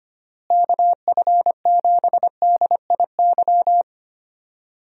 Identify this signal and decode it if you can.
Morse code, heard as a keyed tone, about 25 wpm, 700 Hz. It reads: KF7DIY
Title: USA Call Signs - 6 Characters - Repeated 2x - 25wpm